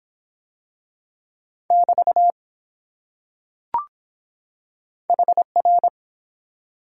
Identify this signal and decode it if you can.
Morse code, heard as a keyed tone, about 23 wpm, 700 Hz. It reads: <BT> HR